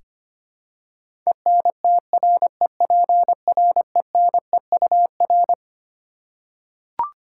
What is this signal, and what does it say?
Morse code, heard as a keyed tone, about 25 wpm, 700 Hz. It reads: ENTREPRENEUR
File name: Twelve Letter Words - 25wpm